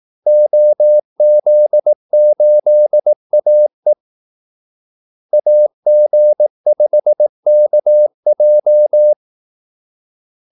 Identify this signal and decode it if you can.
Morse code, heard as a keyed tone, about 18 wpm, 600 Hz. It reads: OZ8AE AG5KJ
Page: Kaldesignaler 18 wpm